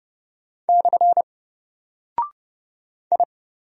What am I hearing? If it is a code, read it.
Morse code, heard as a keyed tone, about 30 wpm, 700 Hz. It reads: / I
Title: Single Character (Letter:Number:Punctuation:Prosign) 30wpm